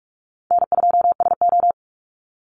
Morse code, heard as a keyed tone, about 44 wpm, 700 Hz. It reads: N2SO
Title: Call Signs - 45wpm